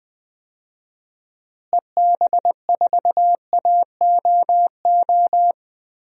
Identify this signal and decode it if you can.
Morse code, heard as a keyed tone, about 20 wpm, 700 Hz. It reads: EB4AOO
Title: DXCC-100 Call Signs - 6 Characters - Repeated 3x - 20wpm